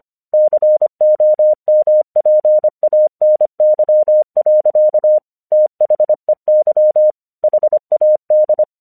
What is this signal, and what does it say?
Morse code, heard as a keyed tone, about 25 wpm, 610 Hz. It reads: COMPANY. THEY HAD
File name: 1 English 25wpm / AQC10